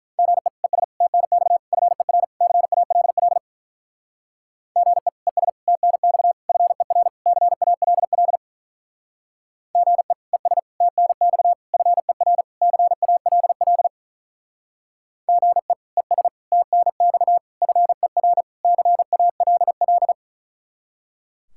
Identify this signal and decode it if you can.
Morse code, heard as a keyed tone, about 44 wpm, 700 Hz. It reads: GE ES TNX FER CALL GE ES TNX FER CALL GE ES TNX FER CALL GE ES TNX FER CALL